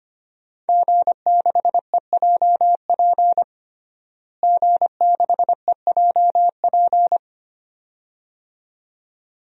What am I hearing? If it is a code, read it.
Morse code, heard as a keyed tone, about 25 wpm, 700 Hz. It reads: G6EJP G6EJP
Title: DXCC-100 Call Signs - 5 Characters - Repeated 2x - 25wpm